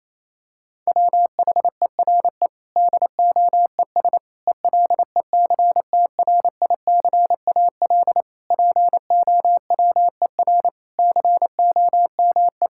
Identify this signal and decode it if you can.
Morse code, heard as a keyed tone, about 28 wpm, 700 Hz. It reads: WHERE DOES ELECTRICAL POWER COME
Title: Sentences from Top 1000 Words - Review Part 6 - 28wpm